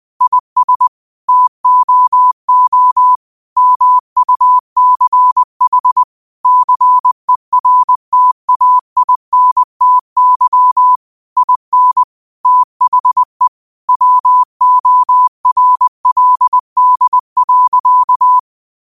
Morse code, heard as a keyed tone, about 20 wpm, 1000 Hz. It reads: IS TOO MUCH CERTAINTY IN THE WORLD.
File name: qotdfiles / 20wpm